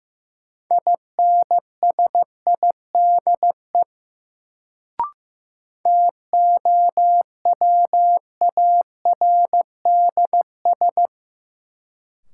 Morse code, heard as a keyed tone, about 15 wpm, 700 Hz. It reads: INSIDE TOWARDS
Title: Sentences from Top 1000 Words - Review Part 2 - New Word Drill - 15wpm